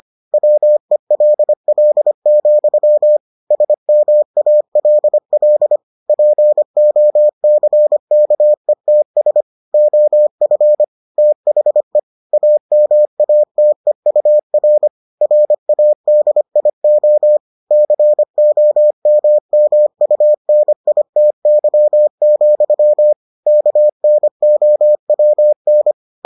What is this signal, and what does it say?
Morse code, heard as a keyed tone, about 25 wpm, 600 Hz. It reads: WELL, SMALL POCKETS OF THE AMATEUR RADIO COMMUNITY, KNOWN